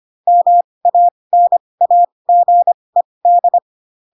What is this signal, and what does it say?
Morse code, heard as a keyed tone, about 25 wpm, spaced as for 20 wpm, 700 Hz. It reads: MANAGED
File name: Pinocchio / Pinocchio ch6 15 25 ik1hge